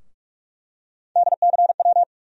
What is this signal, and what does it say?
Morse code, heard as a keyed tone, about 47 wpm, 700 Hz. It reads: DCW
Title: USA Call Sign Postfixes - 3 Letters 45wpm